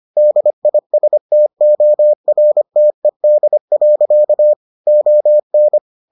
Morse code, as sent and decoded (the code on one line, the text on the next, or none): -.. .. ... - --- .-. - . -.. .-.-.- / --- -.
DISTORTED. ON